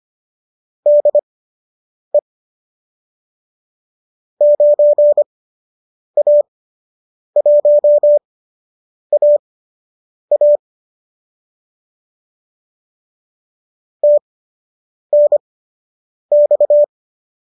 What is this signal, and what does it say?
Morse code, heard as a keyed tone, about 25 wpm, spaced as for 8 wpm, 570 Hz. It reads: DE 9A1AA TNX